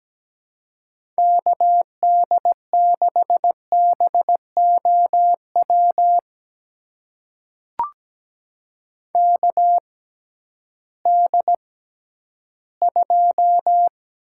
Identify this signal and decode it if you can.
Morse code, heard as a keyed tone, about 17 wpm, spaced as for 11 wpm, 700 Hz. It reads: KD6BOW KD2
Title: USA Call Signs - 6 Characters - 6x Character Spacing - 17wpm